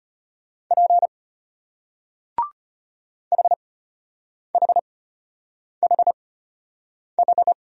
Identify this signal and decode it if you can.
Morse code, heard as a keyed tone, about 34 wpm, spaced as for 8 wpm, 700 Hz. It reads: P HHHH